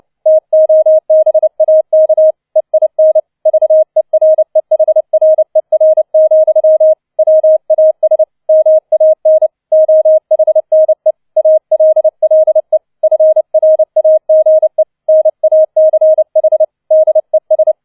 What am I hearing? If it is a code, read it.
Morse code, heard as a keyed tone, about 29 wpm, 630 Hz. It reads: TOBAK EIN VEREHRER, WAS MAN OHNE ALLE FRAGE NACH DES
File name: 004 Vierter Streich 30wpm